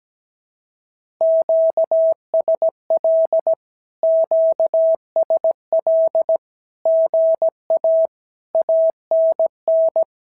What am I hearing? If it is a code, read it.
Morse code, heard as a keyed tone, about 17 wpm, 650 Hz. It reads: QSL QSL GA ANN
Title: SOTA QSO Practice 17wpm